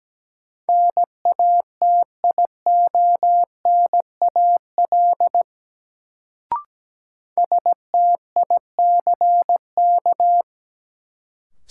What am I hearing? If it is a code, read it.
Morse code, heard as a keyed tone, about 17 wpm, 700 Hz. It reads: NATIONAL STICK
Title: Sentences from Top 1200 Words - New Word Drill Alt - 17wpm